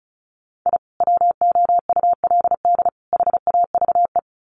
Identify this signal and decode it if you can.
Morse code, heard as a keyed tone, about 34 wpm, 700 Hz. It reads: I WOULD HAVE
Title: Sets of 3 Words 35wpm